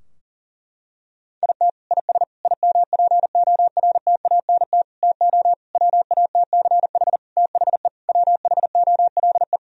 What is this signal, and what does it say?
Morse code, heard as a keyed tone, about 40 wpm, 700 Hz. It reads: IT IS IMPORTANT TO WATCH THE WHOLE
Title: Sentences from Top 1000 Words - Review Part 4 - 40wpm